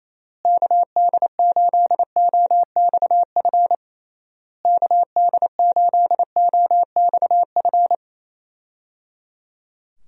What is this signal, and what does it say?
Morse code, heard as a keyed tone, about 28 wpm, 700 Hz. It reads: KD8OXF KD8OXF
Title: USA Call Signs - 6 Characters - Repeated 3x - 28wpm